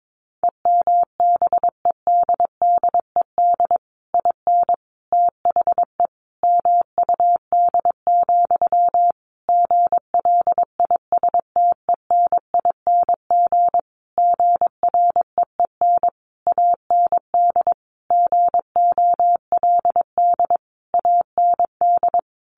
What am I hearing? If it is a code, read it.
Morse code, heard as a keyed tone, about 22 wpm, 700 Hz. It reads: EMBEDDED IN THE MUD, GLISTENING GREEN AND GOLD AND
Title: A Sound Of Thunder 22wpm